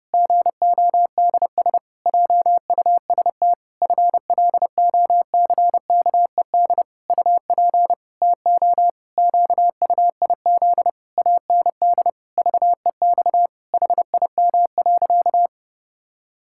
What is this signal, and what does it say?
Morse code, heard as a keyed tone, about 30 wpm, 700 Hz. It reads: GODS JUST FLOCKED UP TO QUIZ AND VEX HIM.
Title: Pangram Challenge v2 - 30wpm